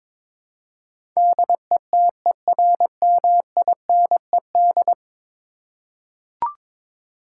Morse code, heard as a keyed tone, about 22 wpm, 700 Hz. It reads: DETERMINED
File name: practice / 22wpm / 10 Letter Words - First 3 Letters Given - 22wpm